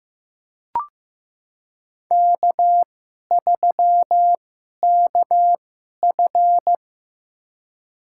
Morse code, heard as a keyed tone, about 15 wpm, spaced as for 11 wpm, 700 Hz. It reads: K3KF
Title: USA Call Signs - 2x Character Spacing - 15wpm